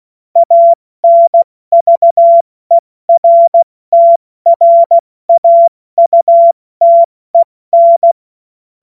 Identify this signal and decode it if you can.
Morse code, heard as a keyed tone, about 16 wpm, 680 Hz. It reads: ANVERTRAUTEN
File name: Morsekurs / Witze / Witz 01. August 2023 70 F80